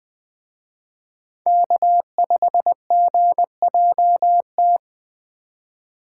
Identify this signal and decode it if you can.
Morse code, heard as a keyed tone, about 19 wpm, 700 Hz. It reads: K5GJT